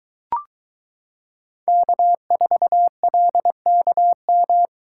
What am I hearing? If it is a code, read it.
Morse code, heard as a keyed tone, about 21 wpm, 700 Hz. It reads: K4LKM